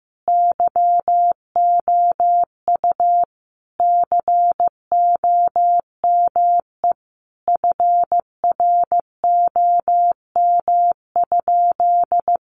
Morse code, heard as a keyed tone, about 15 wpm, 700 Hz. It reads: YOU COME FROM?
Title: Sentences from Top 200 Words 15wpm